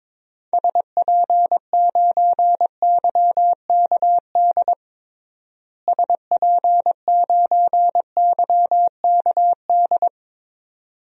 Morse code, heard as a keyed tone, about 21 wpm, 700 Hz. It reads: SP9YKD SP9YKD
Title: DXCC-100 Call Signs - 6 Characters - Repeated 2x - 22wpm